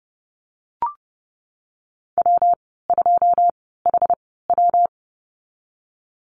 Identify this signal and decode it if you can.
Morse code, heard as a keyed tone, about 30 wpm, spaced as for 17 wpm, 700 Hz. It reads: W2HW
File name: Lesson 19g - Letter w - Call Signs 30wpm